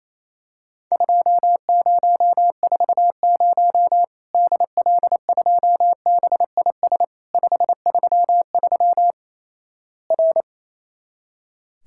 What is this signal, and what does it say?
Morse code, heard as a keyed tone, about 27 wpm, 700 Hz. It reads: 2040 DL2BIS 533 R
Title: Worked All Europe DX Contest - QTC Exchange - 28wpm